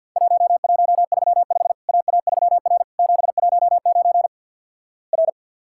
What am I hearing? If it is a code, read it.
Morse code, heard as a keyed tone, about 50 wpm, 700 Hz. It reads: 1135 AA3R 719 R
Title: Worked All Europe DX Contest - QTC Exchange with Repeats - 50wpm